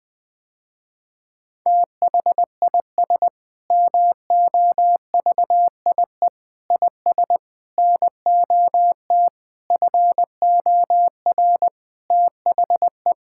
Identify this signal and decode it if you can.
Morse code, heard as a keyed tone, about 20 wpm, 700 Hz. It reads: THIS MOVIE IS NOT FOR THE
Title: Sentences with 7-Word Phrases - Spoken Beginning - 20wpm